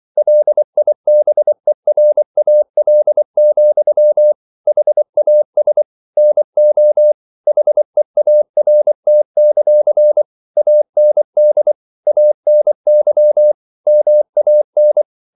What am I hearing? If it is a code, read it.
Morse code, heard as a keyed tone, about 24 wpm, 600 Hz. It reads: LIBERAL, HAS NO HEART; AND ANY MAN
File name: CW / Practice / EnglishQuotations0224